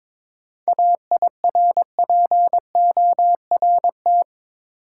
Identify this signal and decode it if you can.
Morse code, heard as a keyed tone, about 22 wpm, 700 Hz. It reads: AIRPORT
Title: Seven Letter Words - 22wpm